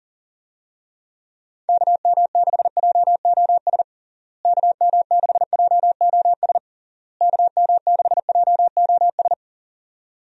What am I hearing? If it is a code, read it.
Morse code, heard as a keyed tone, about 40 wpm, 700 Hz. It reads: KM6JOS KM6JOS KM6JOS
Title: USA Call Signs - 6 Characters - Repeated 3x - 40wpm